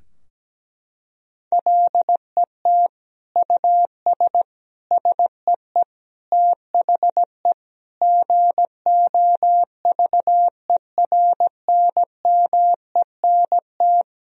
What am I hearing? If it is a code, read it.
Morse code, heard as a keyed tone, about 17 wpm, 700 Hz. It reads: LET US SEE THE GOVERNMENT
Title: Sentences from Top 1000 Words - Review Part 3 - 17wpm